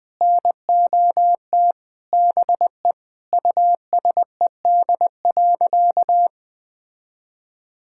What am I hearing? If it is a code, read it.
Morse code, heard as a keyed tone, about 21 wpm, 700 Hz. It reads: NOT BE USED.
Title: Sentences with 9-Word Phrases - Spoken Beginning - 20wpm